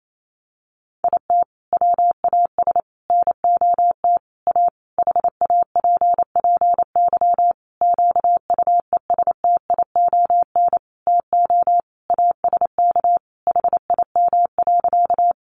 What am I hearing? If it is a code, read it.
Morse code, heard as a keyed tone, about 28 wpm, 700 Hz. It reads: IT WAS NOT A HAPPY QUESTION TO ASK HIM.
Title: Sentences from Top 1000 Words 28wpm